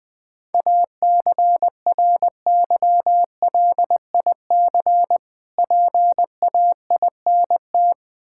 Morse code, heard as a keyed tone, about 20 wpm, 700 Hz. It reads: ACRYLIC PAINT
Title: Difficult Words - Speed-Racing with Hints - 20wpm